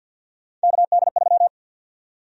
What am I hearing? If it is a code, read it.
Morse code, heard as a keyed tone, about 52 wpm, 700 Hz. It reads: KD3
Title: USA Call Sign Prefixes - 50wpm